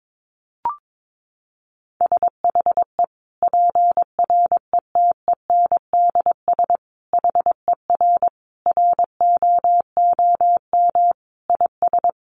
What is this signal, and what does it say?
Morse code, heard as a keyed tone, about 22 wpm, 700 Hz. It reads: SHE PRETENDS HER ROOM IS
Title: Additional Easy Phrases 22wpm